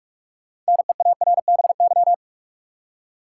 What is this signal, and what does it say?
Morse code, heard as a keyed tone, about 46 wpm, 700 Hz. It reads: NEARBY
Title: Six Letter Words - 45wpm